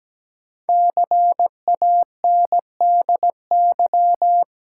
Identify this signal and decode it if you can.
Morse code, heard as a keyed tone, about 17 wpm, 700 Hz. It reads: CANDY
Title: Five Letter Words - 17wpm